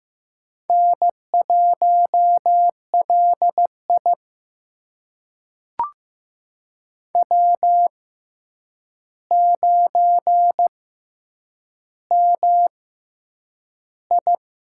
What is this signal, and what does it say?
Morse code, heard as a keyed tone, about 15 wpm, spaced as for 8 wpm, 700 Hz. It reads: N1LI W9MI
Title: USA Call Signs - 4 Characters - 6x Character Spacing - 15wpm